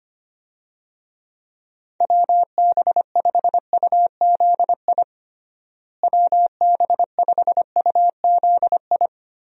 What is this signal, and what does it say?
Morse code, heard as a keyed tone, about 25 wpm, 700 Hz. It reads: WB5UZI WB5UZI
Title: USA Call Signs - 6 Characters - Repeated 3x - 25wpm